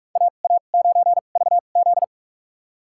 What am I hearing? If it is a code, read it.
Morse code, heard as a keyed tone, about 46 wpm, spaced as for 32 wpm, 700 Hz. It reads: AA9VZ
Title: USA Call Signs - 5 Characters - 2x Character Spacing - 45wpm